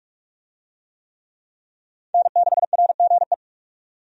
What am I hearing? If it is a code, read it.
Morse code, heard as a keyed tone, about 46 wpm, 700 Hz. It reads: N6RGE
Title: USA Call Signs - 5 Characters - 45wpm